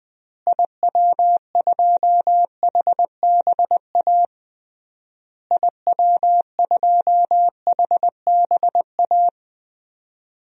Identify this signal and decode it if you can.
Morse code, heard as a keyed tone, about 20 wpm, 700 Hz. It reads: IW2HBA IW2HBA
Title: DXCC-100 Call Signs - 6 Characters - Repeated 3x - 20wpm